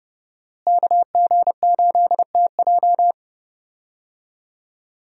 Morse code, heard as a keyed tone, about 30 wpm, 700 Hz. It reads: KG8TJ
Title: USA Call Signs - 5 Characters - Repeated 2x - 30wpm